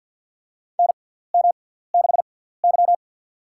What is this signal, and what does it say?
Morse code, heard as a keyed tone, about 53 wpm, spaced as for 17 wpm, 700 Hz. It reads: NM6Y
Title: USA Call Signs - 6x Character Spacing - 50wpm